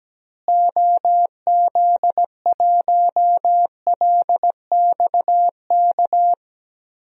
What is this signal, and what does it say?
Morse code, heard as a keyed tone, about 17 wpm, 700 Hz. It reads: OZ1LXK
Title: DXCC-100 Call Signs - 6 Characters - Repeated 2x - 17wpm